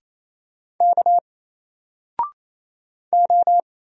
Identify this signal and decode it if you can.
Morse code, heard as a keyed tone, about 28 wpm, 700 Hz. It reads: K O
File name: Single-Letter-Number-28wpm